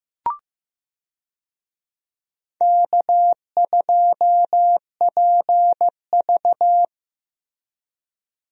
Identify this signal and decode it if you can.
Morse code, heard as a keyed tone, about 15 wpm, 700 Hz. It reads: K2PV
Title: USA Call Signs - 4 Characters - Repeated 3x - 15wpm